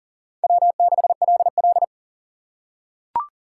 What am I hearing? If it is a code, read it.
Morse code, heard as a keyed tone, about 37 wpm, 700 Hz. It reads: W6LL
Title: USA Call Signs - 4 Characters - 4x Character Spacing - 40wpm